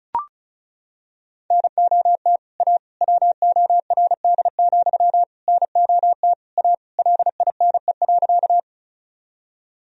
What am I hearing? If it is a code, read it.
Morse code, heard as a keyed tone, about 34 wpm, 690 Hz. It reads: NOT A WORD, NOT A LINE.
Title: Sentences from Top 1000 Words - Review Part 1 - 35wpm